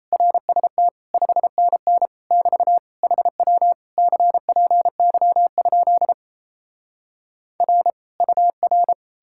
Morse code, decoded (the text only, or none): RST 5NN <BT> HW CPY? R UR